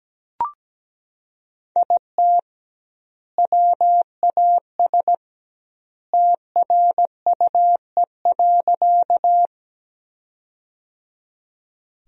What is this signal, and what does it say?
Morse code, heard as a keyed tone, about 17 wpm, 700 Hz. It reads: IT WAS TRUE.
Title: 2x Word-Spacing - Sentences from Top 400 Words - 17wpm